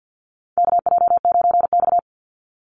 Morse code, heard as a keyed tone, about 50 wpm, 700 Hz. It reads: KJ9X
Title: NA SOTA Chaser Callsigns 50wpm